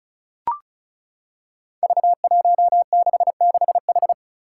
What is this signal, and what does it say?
Morse code, heard as a keyed tone, about 36 wpm, 700 Hz. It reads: V166H
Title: Five Challenging Letter-Number Combinations 35wpm